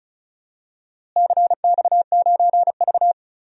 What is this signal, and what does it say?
Morse code, heard as a keyed tone, about 34 wpm, 700 Hz. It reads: CX9V
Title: DXCC-100 Call Signs - 4 Characters - Repeated 3x - 35wpm